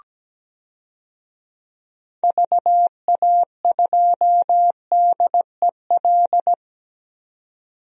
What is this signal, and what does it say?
Morse code, heard as a keyed tone, about 17 wpm, 700 Hz. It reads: VA2DEL
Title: DXCC-100 Call Signs - 6 Characters - 17wpm